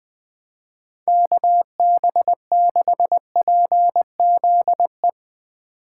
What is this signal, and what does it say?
Morse code, heard as a keyed tone, about 21 wpm, 700 Hz. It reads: KB6PZE
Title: Speed Racing - USA 2x3 Call Signs 20wpm